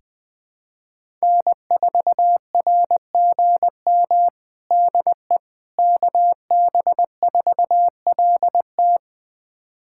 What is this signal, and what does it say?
Morse code, heard as a keyed tone, about 20 wpm, 700 Hz. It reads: N4RGM DE KB4LT
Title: USA Callsign DE Callsign - 5 Characters - 20wpm